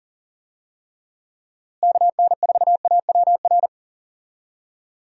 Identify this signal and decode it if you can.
Morse code, heard as a keyed tone, about 40 wpm, 700 Hz. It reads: KN4AWR
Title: USA Call Signs - 6 Characters - Repeated 2x - 40wpm